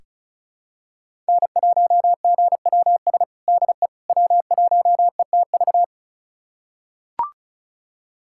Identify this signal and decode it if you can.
Morse code, heard as a keyed tone, about 36 wpm, 700 Hz. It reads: N1GWS DE W1ETV
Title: USA Callsign DE Callsign - 5 Characters - 35wpm